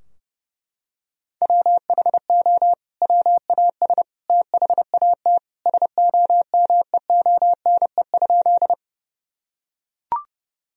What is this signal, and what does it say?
Morse code, heard as a keyed tone, about 30 wpm, 700 Hz. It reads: WHO WAS THAT SOMEONE?
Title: Sentences from Top 1000 Words - Review Part 4 - 30wpm